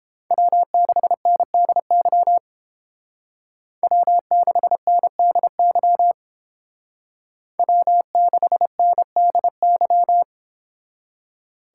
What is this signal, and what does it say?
Morse code, heard as a keyed tone, about 28 wpm, 700 Hz. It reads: W6NDY W6NDY W6NDY